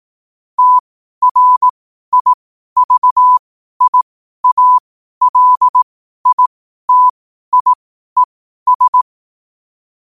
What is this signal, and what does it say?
Morse code, heard as a keyed tone, about 18 wpm, spaced as for 12 wpm, 1000 Hz. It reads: TRIVIALITIES
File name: qotdfiles / 12wpm